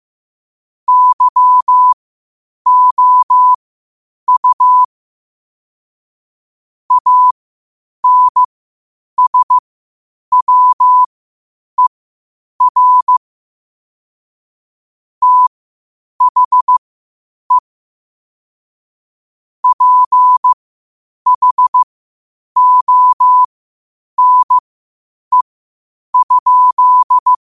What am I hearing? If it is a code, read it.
Morse code, heard as a keyed tone, about 15 wpm, spaced as for 8 wpm, 1000 Hz. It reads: YOU ANSWER THE PHONE?
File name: 8wpm